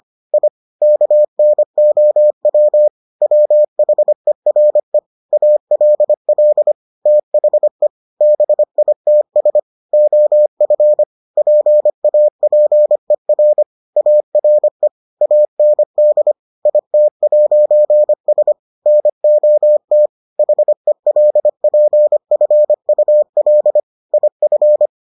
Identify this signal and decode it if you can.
Morse code, heard as a keyed tone, about 25 wpm, 600 Hz. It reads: I KNOW WHERE ALL THE BITS OF PAPER ARE AND IT'S NOT HELPFUL IF